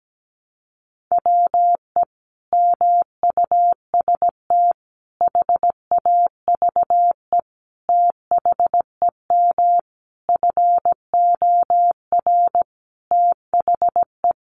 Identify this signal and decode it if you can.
Morse code, heard as a keyed tone, about 17 wpm, 700 Hz. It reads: WE MUST HAVE THEM FOR THE
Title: Sentences from Top 1200 Words 17wpm